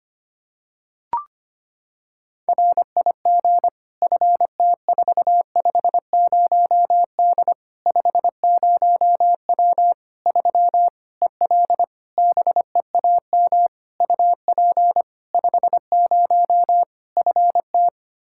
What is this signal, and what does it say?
Morse code, heard as a keyed tone, about 25 wpm, 700 Hz. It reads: RIG FT450D 50W 3 EL BEAM UP 50 FT
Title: K1USN SST QSO Practice 25wpm